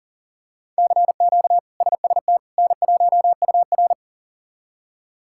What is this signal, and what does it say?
Morse code, heard as a keyed tone, about 40 wpm, 700 Hz. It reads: CQ SST N1UR
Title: K1USN SST QSO Practice - 40wpm